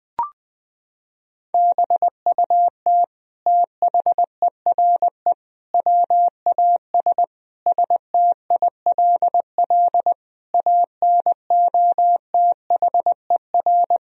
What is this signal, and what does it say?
Morse code, heard as a keyed tone, about 20 wpm, 700 Hz. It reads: BUT THERE WAS STILL ANOTHER
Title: Sentences from Top 1000 Words - Review Part 2 - 20wpm